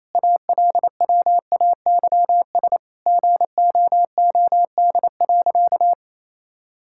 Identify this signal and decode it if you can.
Morse code, heard as a keyed tone, about 28 wpm, 700 Hz. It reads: ALWAYS GOOD.
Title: Sentences from Top 1000 Words - Review Part 5 - 28wpm